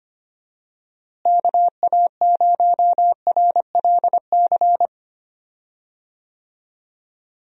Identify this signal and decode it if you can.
Morse code, heard as a keyed tone, about 25 wpm, 700 Hz. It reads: KA0RLC
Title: USA Call Signs - 6 Characters - Repeated 3x - 25wpm